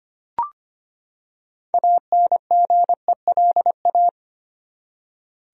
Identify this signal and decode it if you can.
Morse code, heard as a keyed tone, about 25 wpm, 700 Hz. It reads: ANGELA
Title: USA Common Names - 25wpm